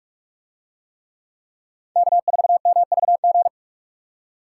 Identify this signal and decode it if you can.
Morse code, heard as a keyed tone, about 40 wpm, 700 Hz. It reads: K4MVG